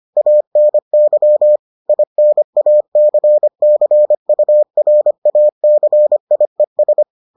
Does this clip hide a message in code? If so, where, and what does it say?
Yes: Morse code from 0.2 to 7.0 s, heard as a keyed tone, about 25 wpm, 600 Hz. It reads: ANY INACCURACIES